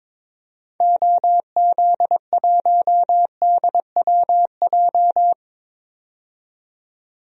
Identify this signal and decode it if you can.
Morse code, heard as a keyed tone, about 22 wpm, 700 Hz. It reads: OZ1DWJ
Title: DXCC-100 Call Signs - 6 Characters - 22wpm